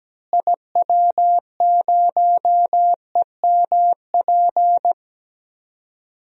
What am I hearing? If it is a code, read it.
Morse code, heard as a keyed tone, about 17 wpm, 700 Hz. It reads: IW0EMP
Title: DXCC-100 Call Signs - 6 Characters - Repeated 3x - 17wpm